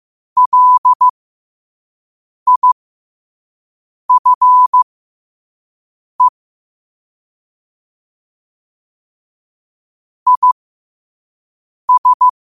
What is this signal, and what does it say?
Morse code, heard as a keyed tone, about 15 wpm, spaced as for 5 wpm, 1000 Hz. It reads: LIFE IS
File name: qotdfiles / 5wpm